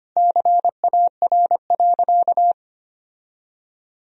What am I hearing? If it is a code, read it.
Morse code, heard as a keyed tone, about 25 wpm, 700 Hz. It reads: CAR.
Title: Sentences from Top 1000 Words - Review Part 1 - 25wpm